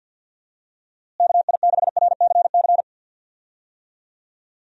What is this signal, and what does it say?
Morse code, heard as a keyed tone, about 51 wpm, 700 Hz. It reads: KI6RCC
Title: USA Call Signs - 6 Characters - Repeated 2x - 50wpm